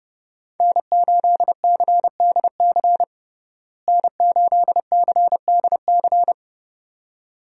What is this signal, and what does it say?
Morse code, heard as a keyed tone, about 31 wpm, 700 Hz. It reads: N8CDC N8CDC
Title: USA Call Signs - 5 Characters - Repeated 2x - 30wpm